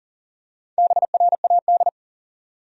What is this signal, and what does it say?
Morse code, heard as a keyed tone, about 40 wpm, 700 Hz. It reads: BRAD
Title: USA Common Names - Max 4 Letters - 40wpm